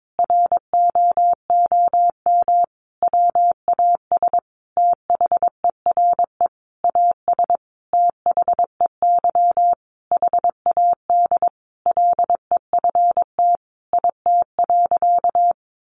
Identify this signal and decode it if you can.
Morse code, heard as a keyed tone, about 22 wpm, 700 Hz. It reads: ROOM WAS THERE AS THEY HAD LEFT IT.
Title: A Sound Of Thunder 22wpm